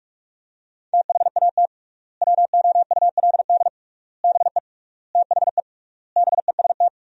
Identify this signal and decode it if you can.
Morse code, heard as a keyed tone, about 48 wpm, 700 Hz. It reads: THAT WOULD BE THE BEST
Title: 3x Word-Spacing - Sentences from Top 400 Words - 45wpm